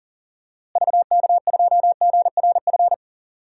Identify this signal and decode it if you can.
Morse code, heard as a keyed tone, about 40 wpm, 700 Hz. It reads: VK2GRF
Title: DXCC-100 Call Signs - 6 Characters - Repeated 3x - 40wpm